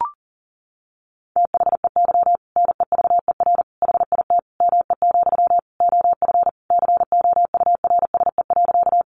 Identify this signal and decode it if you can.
Morse code, heard as a keyed tone, about 40 wpm, 700 Hz. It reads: THEY NEVER HIT ME, OF COURSE.
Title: Sentences from Top 1200 Words 40wpm